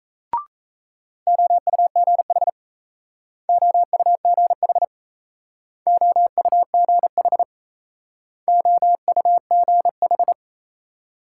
Morse code, heard as a keyed tone, about 36 wpm, 700 Hz. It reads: OUGH OUGH OUGH OUGH